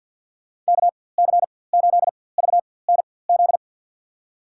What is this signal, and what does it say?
Morse code, heard as a keyed tone, about 50 wpm, spaced as for 23 wpm, 700 Hz. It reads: KC8VNZ
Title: USA Call Signs - 6 Characters - 4x Character Spacing - 50wpm